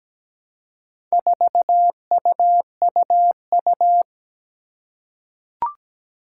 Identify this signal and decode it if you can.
Morse code, heard as a keyed tone, about 17 wpm, 700 Hz. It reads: 4UUU
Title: Four Challenging Letter-Number Combinations 17wpm